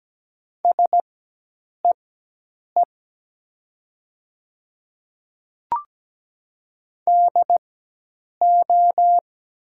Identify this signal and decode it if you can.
Morse code, heard as a keyed tone, about 17 wpm, spaced as for 8 wpm, 700 Hz. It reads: SEE DO
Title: 4x Character Spacing - Top 300 Words - 17wpm